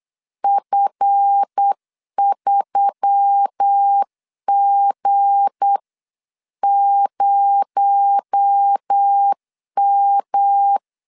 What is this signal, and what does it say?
Morse code, heard as a keyed tone, about 8 wpm, 800 Hz. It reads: F3G0M
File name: CW / DET / 1 9 1500C S A 8wpm 5637